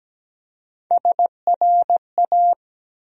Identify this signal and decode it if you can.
Morse code, heard as a keyed tone, about 17 wpm, 700 Hz. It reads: SRA
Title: USA Call Sign Postfixes - 3 Letters 17wpm